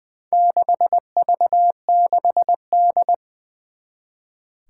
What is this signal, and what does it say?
Morse code, heard as a keyed tone, about 20 wpm, 700 Hz. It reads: 6V6D
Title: Four Challenging Letter-Number Combinations 20wpm